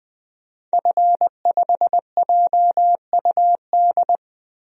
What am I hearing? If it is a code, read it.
Morse code, heard as a keyed tone, about 20 wpm, 700 Hz. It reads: F5JUD
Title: DXCC-100 Call Signs - 5 Characters - 20wpm